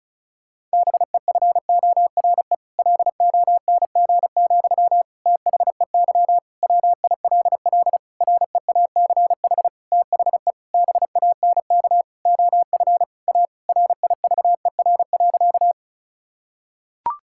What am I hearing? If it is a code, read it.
Morse code, heard as a keyed tone, about 36 wpm, 700 Hz. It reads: BEFORE LONG, THEY WILL REACH THE BANK OF A RIVER.
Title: Sentences from Top 1000 Words - Review Part 6 - 35wpm